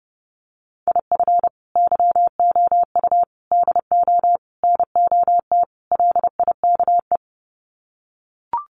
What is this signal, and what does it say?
Morse code, heard as a keyed tone, about 30 wpm, 700 Hz. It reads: IF YOU DO NOT LIKE
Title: Sets of 5 Words 30wpm